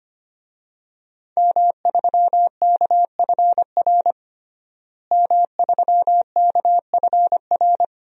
Morse code, heard as a keyed tone, about 25 wpm, 700 Hz. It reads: M3KFR M3KFR
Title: DXCC-100 Call Signs - 5 Characters - Repeated 2x - 25wpm